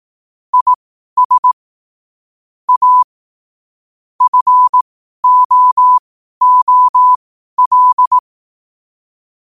Morse code, heard as a keyed tone, about 18 wpm, spaced as for 12 wpm, 1000 Hz. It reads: IS A FOOL
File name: qotdfiles / 12wpm